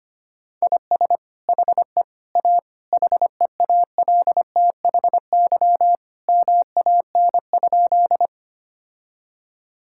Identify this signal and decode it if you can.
Morse code, heard as a keyed tone, about 25 wpm, 700 Hz. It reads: IS HE A HEALTHY MAN?
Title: Sentences from Top 1000 Words - Review Part 5 - 25wpm